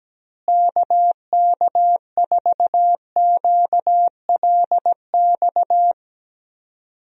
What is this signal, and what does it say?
Morse code, heard as a keyed tone, about 17 wpm, 700 Hz. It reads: KK4QLX
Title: USA Call Signs - 6 Characters - Repeated 3x - 17wpm